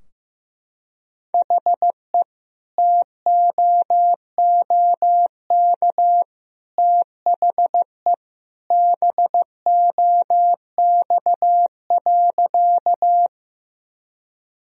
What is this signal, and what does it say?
Morse code, heard as a keyed tone, about 15 wpm, 700 Hz. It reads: HE TOOK THE BOX.
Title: Sentences from Top 1000 Words - Review Part 2 - 15wpm